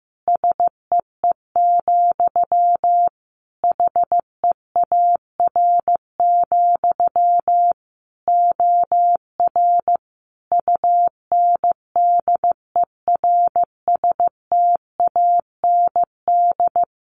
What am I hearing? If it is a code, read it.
Morse code, heard as a keyed tone, about 15 wpm, 700 Hz. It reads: SEE, HEAR, OR UNDERSTAND
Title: Sentences from Top 1300 Words 15wpm